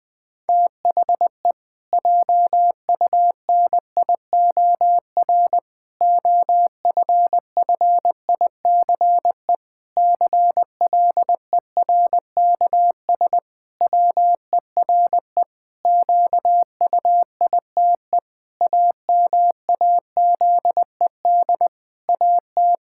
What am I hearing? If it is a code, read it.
Morse code, heard as a keyed tone, about 21 wpm, 700 Hz. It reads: THE JUNIOR OFFICE CLERKS WERE QUITE AMAZED AT
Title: Pangram Challenge v2 - 20wpm